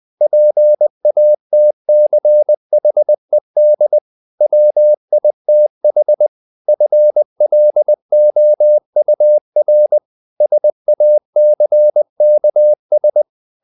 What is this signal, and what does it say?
Morse code, heard as a keyed tone, about 20 wpm, 600 Hz. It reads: PATCHED WITH FLOUR SACKS
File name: the old man and the sea / the old man and the sea part 1 20wpm